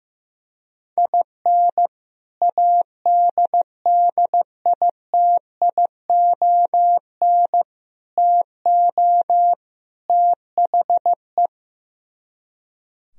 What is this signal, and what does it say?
Morse code, heard as a keyed tone, about 15 wpm, 700 Hz. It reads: IN ADDITION TO THE
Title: Speed-Racing - Sets of 4 Words 15wpm